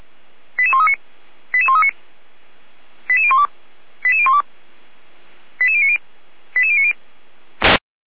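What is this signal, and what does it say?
Five-tone selective call (ZVEI), heard as quick runs of five tones, tones of about 70 ms. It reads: 88139 88139 88012 88012 88090 88090
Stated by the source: ZVEI five-tone selective call received on SDR, fire brigade Orlamünde (feuerwehr-orlamuende).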